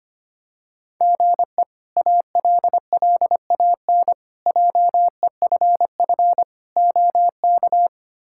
Morse code, heard as a keyed tone, about 25 wpm, 700 Hz. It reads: GE ALLAN JEFF OK
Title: K1USN SST QSO Practice - 25wpm